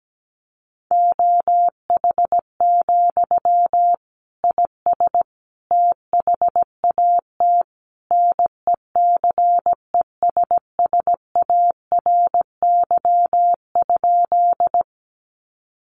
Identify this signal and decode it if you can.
Morse code, heard as a keyed tone, about 17 wpm, 700 Hz. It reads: OH, IS THAT NECESSARY?
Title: Sentences from Top 800 Words 17wpm